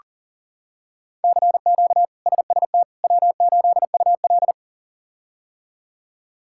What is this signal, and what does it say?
Morse code, heard as a keyed tone, about 40 wpm, 700 Hz. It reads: CQ SST W8UL
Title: K1USN SST QSO Practice - 40wpm